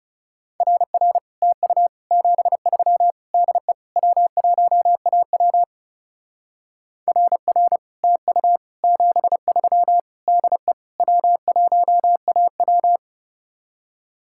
Morse code, heard as a keyed tone, about 35 wpm, 700 Hz. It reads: RR TU 73 DE W1AW RR TU 73 DE W1AW
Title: Speed-Racing Common QSO Phrases - 30wpm